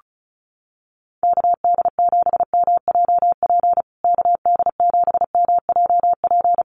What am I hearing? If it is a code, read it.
Morse code, heard as a keyed tone, about 34 wpm, 700 Hz. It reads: KD7MJP KD7MJP
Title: USA Call Signs Repeated 3x 35wpm